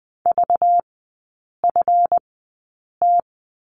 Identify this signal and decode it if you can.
Morse code, heard as a keyed tone, about 20 wpm, spaced as for 8 wpm, 700 Hz. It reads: VFT
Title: 191 Lesson 31e - Letter v - Three Characters 20wpm